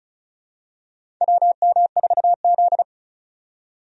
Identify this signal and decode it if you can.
Morse code, heard as a keyed tone, about 34 wpm, 700 Hz. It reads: WM4Z
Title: USA Call Signs - 4 Characters - 35wpm